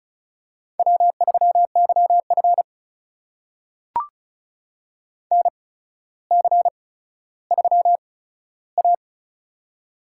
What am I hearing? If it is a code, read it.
Morse code, heard as a keyed tone, about 36 wpm, spaced as for 15 wpm, 700 Hz. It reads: W3YF NC3A
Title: USA Call Signs - 4 Characters - 8x Character Spacing - 35wpm